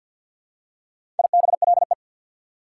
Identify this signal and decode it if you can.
Morse code, heard as a keyed tone, about 48 wpm, 690 Hz. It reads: IBLE
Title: Prefixes and Suffixes 50wpm